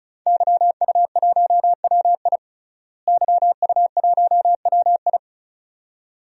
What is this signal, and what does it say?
Morse code, heard as a keyed tone, about 36 wpm, 700 Hz. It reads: YU1WI YU1WI
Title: DXCC-100 Call Signs - 5 Characters - Repeated 2x - 35wpm